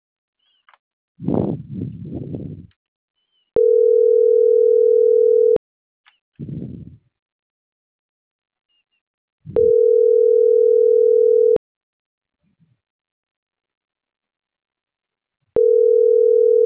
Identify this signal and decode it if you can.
Ringback tone (North America), 440 + 480 Hz.